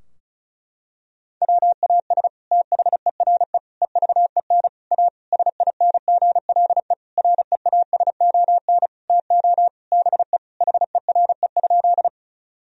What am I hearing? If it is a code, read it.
Morse code, heard as a keyed tone, about 36 wpm, 700 Hz. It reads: WAS THERE EVEN A SINGLE REASON TO BE HERE?
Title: Sentences from Top 1000 Words - Review Part 4 - 35wpm